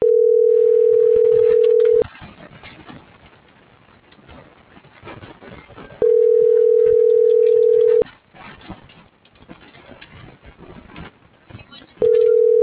Ringback tone (North America), 440 + 480 Hz.